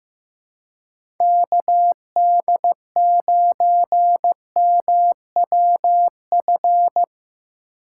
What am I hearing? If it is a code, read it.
Morse code, heard as a keyed tone, about 17 wpm, 700 Hz. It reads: KD9MWF
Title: Speed Racing - USA 2x3 Call Signs 15wpm